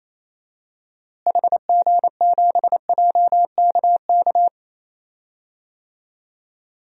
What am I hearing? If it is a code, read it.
Morse code, heard as a keyed tone, about 28 wpm, 700 Hz. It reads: HG7JKK
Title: DXCC-100 Call Signs - 6 Characters - Repeated 3x - 28wpm